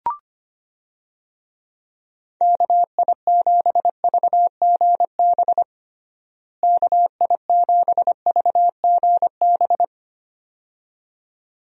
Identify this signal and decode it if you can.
Morse code, heard as a keyed tone, about 25 wpm, 700 Hz. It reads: KI7VGB KI7VGB
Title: USA Call Signs - 6 Characters - Repeated 2x - 25wpm